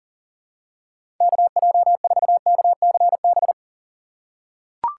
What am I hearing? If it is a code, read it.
Morse code, heard as a keyed tone, about 40 wpm, 700 Hz. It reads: KJ4KCB
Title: USA Call Signs - 6 Characters - 4x Character Spacing - 40wpm